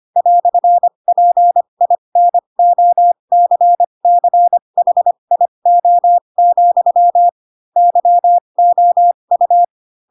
Morse code, heard as a keyed tone, about 25 wpm, spaced as for 22 wpm, 700 Hz. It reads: "PINOCCHIO, YOU
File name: Pinocchio / Pinocchio ch3 18 25 ik1hge